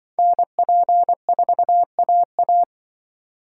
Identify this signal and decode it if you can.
Morse code, heard as a keyed tone, about 24 wpm, 700 Hz. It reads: NP4AA
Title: Speed Racing - USA 2x2 Call Signs 20wpm